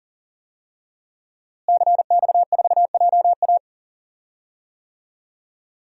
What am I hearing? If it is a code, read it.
Morse code, heard as a keyed tone, about 40 wpm, 700 Hz. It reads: CX4JA
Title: DXCC-100 Call Signs - 5 Characters - Repeated 2x - 40wpm